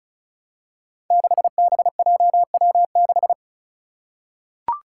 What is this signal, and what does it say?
Morse code, heard as a keyed tone, about 36 wpm, 700 Hz. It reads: 6BJW6
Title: Five Challenging Letter-Number Combinations 35wpm